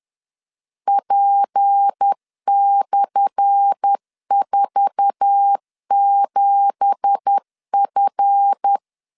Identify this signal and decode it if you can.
Morse code, heard as a keyed tone, about 10 wpm, 800 Hz. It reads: P/47F
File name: LEZIONI / 10 SIGNS / 10A SLASH 10wpm 2808